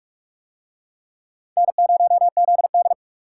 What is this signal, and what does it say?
Morse code, heard as a keyed tone, about 44 wpm, 700 Hz. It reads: N0ZD